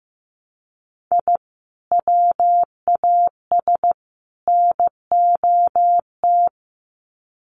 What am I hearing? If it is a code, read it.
Morse code, heard as a keyed tone, about 15 wpm, 700 Hz. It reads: I WAS NOT
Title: Sets of 3 Words 15wpm